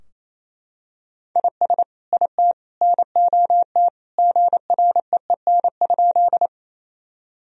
Morse code, heard as a keyed tone, about 28 wpm, 700 Hz. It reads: IS IT NOT GREEN?
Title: Sentences from Top 1000 Words - Review Part 2 - 28wpm